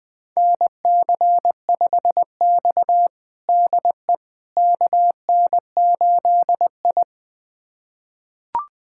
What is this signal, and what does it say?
Morse code, heard as a keyed tone, about 20 wpm, 700 Hz. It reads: NC5X DE KN8I
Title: USA Callsign DE Callsign - 4 Characters - 20wpm